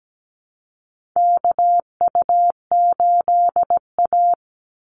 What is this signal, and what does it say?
Morse code, heard as a keyed tone, about 17 wpm, 700 Hz. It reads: KU8A
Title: USA Call Signs 17wpm